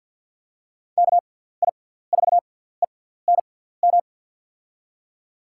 Morse code, heard as a keyed tone, about 48 wpm, spaced as for 17 wpm, 700 Hz. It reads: KI4ENM
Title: USA Call Signs - 6 Characters - 6x Character Spacing - 50wpm